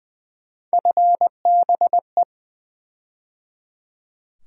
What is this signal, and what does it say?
Morse code, heard as a keyed tone, about 20 wpm, 700 Hz. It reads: FBE
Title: USA Call Sign Postfixes - 3 Letters 20wpm